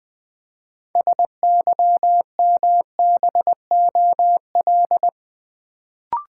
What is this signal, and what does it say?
Morse code, heard as a keyed tone, about 20 wpm, 690 Hz. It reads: SYMBOL
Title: Build A Word - 2 Letters At A Time - 6-Letter Words - 20wpm